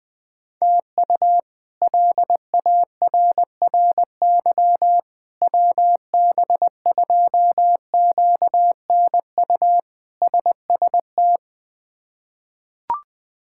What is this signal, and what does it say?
Morse code, heard as a keyed tone, about 20 wpm, 700 Hz. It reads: TU LARRY WB2QNU SST
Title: K1USN SST QSO Practice - 20wpm